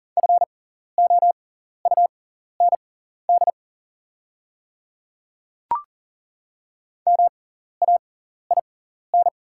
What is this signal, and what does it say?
Morse code, heard as a keyed tone, about 40 wpm, spaced as for 14 wpm, 690 Hz. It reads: FOUND MAIN
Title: 6x Character Spacing - Top 300 Words - 40wpm